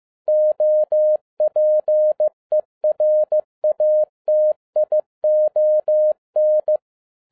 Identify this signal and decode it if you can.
Morse code, heard as a keyed tone, about 15 wpm, 610 Hz. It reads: OPERATION
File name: 2012-01-mp3 / n07-ARA JR0DIL-15wpm